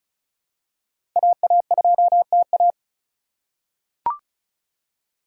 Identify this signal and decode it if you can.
Morse code, heard as a keyed tone, about 34 wpm, 700 Hz. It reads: AA2TA
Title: USA Call Signs - 8x Character Spacing - 35wpm